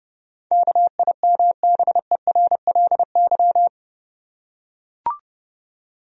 Morse code, heard as a keyed tone, about 30 wpm, 700 Hz. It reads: KIMBERLY
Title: USA Common Names - 30wpm